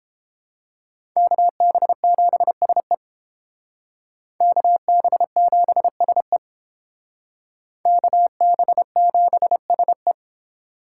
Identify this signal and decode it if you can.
Morse code, heard as a keyed tone, about 28 wpm, 700 Hz. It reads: KB7SE KB7SE KB7SE